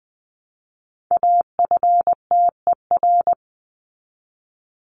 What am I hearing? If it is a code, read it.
Morse code, heard as a keyed tone, about 20 wpm, 700 Hz. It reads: AFTER
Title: Top 300 Words Repeated 20wpm